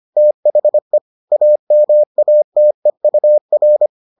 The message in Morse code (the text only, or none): THE AMATEUR